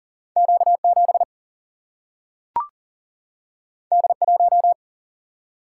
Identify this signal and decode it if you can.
Morse code, heard as a keyed tone, about 40 wpm, 700 Hz. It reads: Q7 D1
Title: Single-Letter Single-Number 40wpm